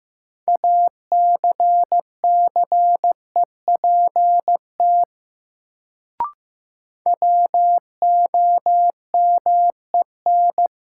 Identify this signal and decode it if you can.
Morse code, heard as a keyed tone, about 15 wpm, 700 Hz. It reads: ACCEPT WOMEN
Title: Sentences from Top 1300 Words - New Word Drill - 15wpm